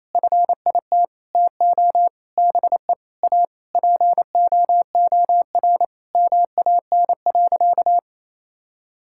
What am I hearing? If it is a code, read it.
Morse code, heard as a keyed tone, about 28 wpm, 700 Hz. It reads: FIT TO BE A POOR MAN.
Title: Sentences from Top 1000 Words - Review Part 5 - 28wpm